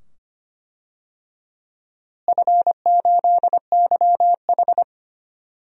Morse code, heard as a keyed tone, about 25 wpm, 700 Hz. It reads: F8YH
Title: DXCC-100 Call Signs - 4 Characters - Repeated 2x - 25wpm